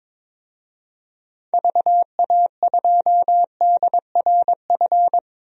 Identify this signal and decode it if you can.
Morse code, heard as a keyed tone, about 22 wpm, 700 Hz. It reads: VA2DRF
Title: DXCC-100 Call Signs - 6 Characters - Repeated 3x - 22wpm